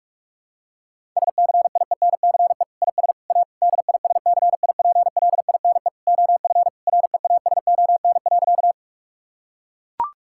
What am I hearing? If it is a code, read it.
Morse code, heard as a keyed tone, about 41 wpm, 700 Hz. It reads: SCIENCE IS A DISCIPLINE OF REASON.
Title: Sentences from Top 1000 Words - Review Part 6 - 45wpm